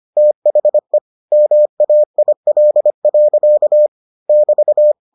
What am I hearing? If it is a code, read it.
Morse code, heard as a keyed tone, about 25 wpm, 600 Hz. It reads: THE MAIL. <BT>